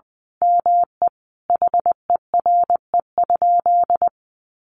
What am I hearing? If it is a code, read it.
Morse code, heard as a keyed tone, about 20 wpm, 700 Hz. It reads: ME HERE?
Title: Sentences from Top 1200 Words 20wpm